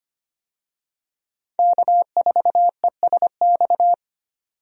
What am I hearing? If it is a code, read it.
Morse code, heard as a keyed tone, about 25 wpm, 700 Hz. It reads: K4ESX
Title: USA Call Signs - 5 Characters - 25wpm